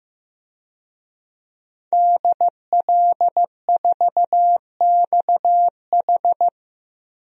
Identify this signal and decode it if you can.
Morse code, heard as a keyed tone, about 15 wpm, 700 Hz. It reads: DL4XH
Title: DXCC-100 Call Signs - 5 Characters - Repeated 3x - 15wpm